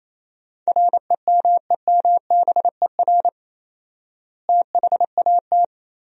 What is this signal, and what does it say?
Morse code, heard as a keyed tone, about 28 wpm, 700 Hz. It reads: REMEMBER THAT
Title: 4x Word-Spacing - Sentences from Top 400 Words - 28wpm